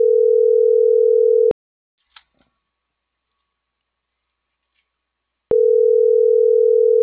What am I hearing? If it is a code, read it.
Ringback tone (North America), 440 + 480 Hz.